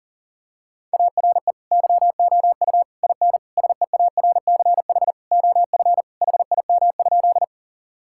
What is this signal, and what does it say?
Morse code, heard as a keyed tone, about 37 wpm, 700 Hz. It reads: ARE YOU IN SEARCH OF HIM?
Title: Sentences from Top 1000 Words - Review Part 5 - 40wpm